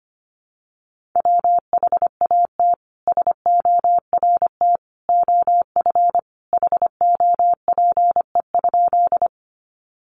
Morse code, heard as a keyed tone, about 25 wpm, 700 Hz. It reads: WHAT SORT OF HOPE?
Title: Sentences from Top 1300 Words 25wpm